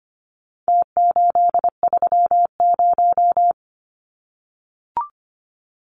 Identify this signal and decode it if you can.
Morse code, heard as a keyed tone, about 25 wpm, 700 Hz. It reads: T830
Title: Numbers - Contest Style - 4-Digit 25wpm